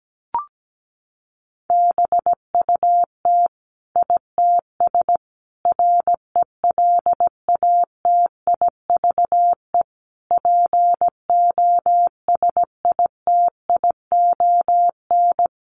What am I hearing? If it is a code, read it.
Morse code, heard as a keyed tone, about 17 wpm, 700 Hz. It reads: BUT ITS RELATIVE POSITION
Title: Sentences from Top 1400 Words 17wpm